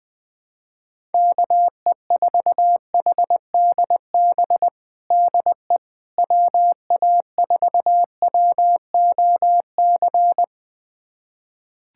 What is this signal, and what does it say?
Morse code, heard as a keyed tone, about 21 wpm, 700 Hz. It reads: KE4HDB DE WA4WOC
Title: USA Callsign DE Callsign - 6 Characters - 20wpm